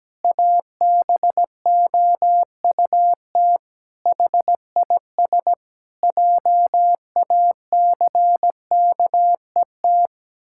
Morse code, heard as a keyed tone, about 17 wpm, 700 Hz. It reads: ABOUT HIS JACKET
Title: Pangram Challenge v2 - 17wpm